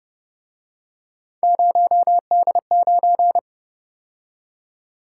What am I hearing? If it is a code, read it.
Morse code, heard as a keyed tone, about 31 wpm, 700 Hz. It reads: 0D9